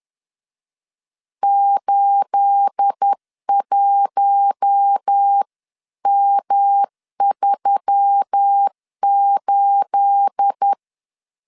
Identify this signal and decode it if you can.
Morse code, heard as a keyed tone, about 10 wpm, 800 Hz. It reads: 81M38
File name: CW / DET / 1 9 1500C S A 10wpm 4511